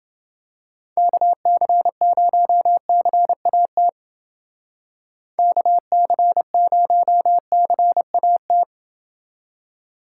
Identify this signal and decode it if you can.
Morse code, heard as a keyed tone, about 28 wpm, 700 Hz. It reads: KC0CAT KC0CAT